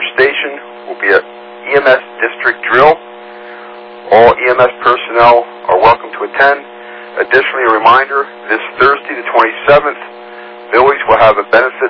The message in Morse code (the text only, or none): none